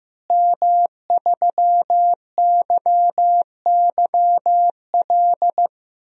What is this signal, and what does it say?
Morse code, heard as a keyed tone, about 15 wpm, 700 Hz. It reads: M3YYL
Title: DXCC-100 Call Signs - 5 Characters - Repeated 2x - 15wpm